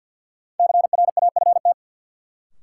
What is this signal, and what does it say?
Morse code, heard as a keyed tone, about 49 wpm, 700 Hz. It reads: CRAFT
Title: Sentences from Top 1000 Words - Review Part 6 - New Word Drill - 50wpm